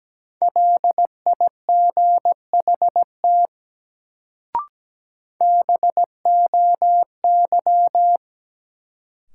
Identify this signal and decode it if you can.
Morse code, heard as a keyed tone, about 17 wpm, 700 Hz. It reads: LIGHT BOY
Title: Sentences from Top 200 Words - New Word Drill - 17wpm